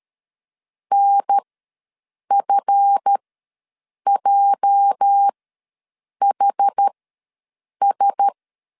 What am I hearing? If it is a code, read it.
Morse code, heard as a keyed tone, about 13 wpm, spaced as for 7 wpm, 800 Hz. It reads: NFJHS
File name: LEZIONI / 8 YJ45 / 8 YJ ws 12wpm 3342